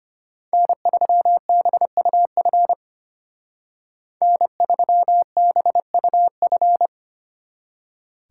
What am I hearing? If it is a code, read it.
Morse code, heard as a keyed tone, about 33 wpm, 700 Hz. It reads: N3BUF N3BUF